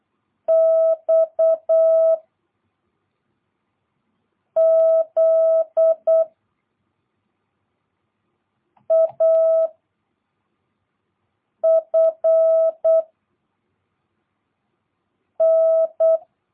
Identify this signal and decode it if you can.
Morse code, heard as a keyed tone, about 8 wpm, 660 Hz. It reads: X Z A F N